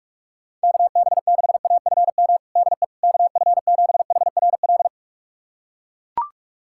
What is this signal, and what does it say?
Morse code, heard as a keyed tone, about 41 wpm, 690 Hz. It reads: KB6AFM DE KF7HRL
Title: USA Callsign DE Callsign - 6 Characters - 45wpm